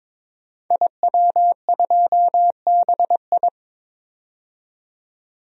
Morse code, heard as a keyed tone, about 22 wpm, 700 Hz. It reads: IW2BI
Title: DXCC-100 Call Signs - 5 Characters - 22wpm